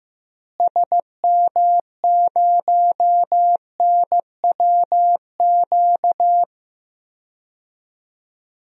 Morse code, heard as a keyed tone, about 15 wpm, 700 Hz. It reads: SM0NWQ
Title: DXCC-100 Call Signs - 6 Characters - Repeated 3x - 15wpm